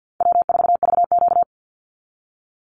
Morse code, heard as a keyed tone, about 48 wpm, 700 Hz. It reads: W4VQ
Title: Easy Call Signs Repeated 2x - 50wpm